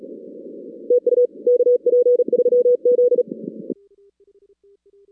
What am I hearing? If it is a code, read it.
Morse code, heard as a keyed tone, about 38 wpm, 500 Hz. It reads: TU KP3Z
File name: 2016 arrl dx cw / 2016-ARRL-DX-CW-K5TR- - 0-17